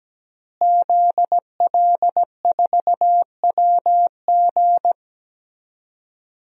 Morse code, heard as a keyed tone, about 17 wpm, 700 Hz. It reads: ZL4WG
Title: DXCC-100 Call Signs - 5 Characters - Repeated 3x - 17wpm